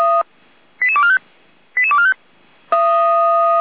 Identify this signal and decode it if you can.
Five-tone selective call (ZVEI), heard as quick runs of five tones, tones of about 70 ms. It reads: 88245 88245
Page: /455/SDR/ZVEI/88243/